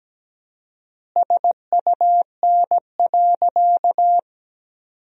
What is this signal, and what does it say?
Morse code, heard as a keyed tone, about 17 wpm, 700 Hz. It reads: SUN.
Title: 4x Word-Spacing - Sentences from Top 500 Words - 17wpm